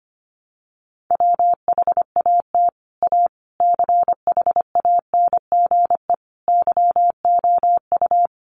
Morse code, heard as a keyed tone, about 25 wpm, 700 Hz. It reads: WHAT A CHANGE YOU
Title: Sentences from Top 600 Words 25wpm